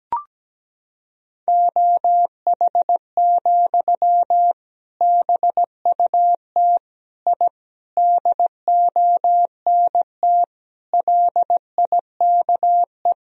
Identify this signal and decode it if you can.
Morse code, heard as a keyed tone, about 17 wpm, 700 Hz. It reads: OH, BUT I DONT LIKE
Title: Sentences from Top 1000 Words - Review Part 3 - 17wpm